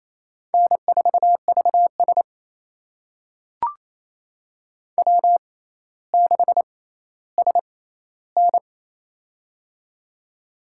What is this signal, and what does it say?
Morse code, heard as a keyed tone, about 28 wpm, spaced as for 14 wpm, 700 Hz. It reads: N4VS W6SN
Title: USA Call Signs - 4 Characters - 6x Character Spacing - 28wpm